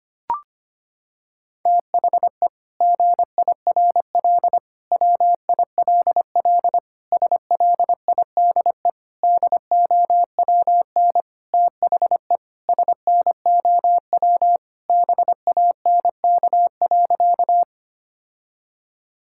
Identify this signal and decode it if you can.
Morse code, heard as a keyed tone, about 25 wpm, 700 Hz. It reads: THE GIRL WILL SLIDE DOWN THE SNOW BANK.
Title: Sentences from Top 1600 Words 25wpm